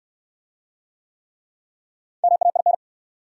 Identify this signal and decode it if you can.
Morse code, heard as a keyed tone, about 68 wpm, 700 Hz. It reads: MAIN